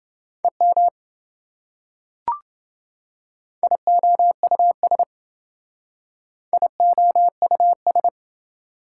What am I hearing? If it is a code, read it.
Morse code, heard as a keyed tone, about 28 wpm, 700 Hz. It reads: EM IOUS IOUS